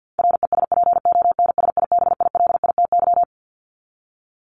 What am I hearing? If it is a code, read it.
Morse code, heard as a keyed tone, about 50 wpm, 700 Hz. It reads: RESPONSIBILITY
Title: Top 1001-1250 Words 50wpm